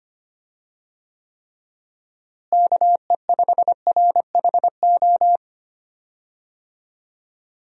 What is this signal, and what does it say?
Morse code, heard as a keyed tone, about 25 wpm, 700 Hz. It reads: KE5RHO
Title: USA Call Signs - 6 Characters - Repeated 3x - 25wpm